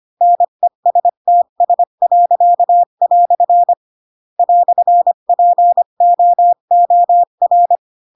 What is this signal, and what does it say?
Morse code, heard as a keyed tone, about 25 wpm, 700 Hz. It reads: NESTS." "POOR
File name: Pinocchio / Pinocchio ch4 20 25 ik1hge